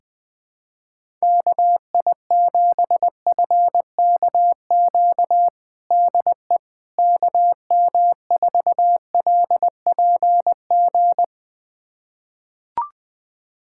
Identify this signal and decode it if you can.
Morse code, heard as a keyed tone, about 21 wpm, 700 Hz. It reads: KI7FKQ DE KM4LPG
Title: USA Callsign DE Callsign - 6 Characters - 20wpm